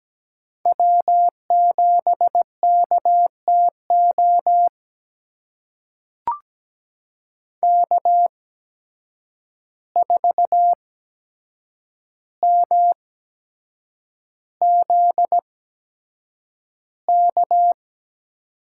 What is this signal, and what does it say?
Morse code, heard as a keyed tone, about 17 wpm, spaced as for 7 wpm, 700 Hz. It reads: W7KTO K4MZK
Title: USA Call Signs - 5 Characters - 8x Character Spacing - 17wpm